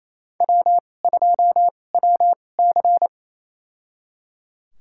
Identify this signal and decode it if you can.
Morse code, heard as a keyed tone, about 28 wpm, spaced as for 20 wpm, 700 Hz. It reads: W2WC
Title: USA Call Signs - 4 Characters - 2x Character Spacing - 28wpm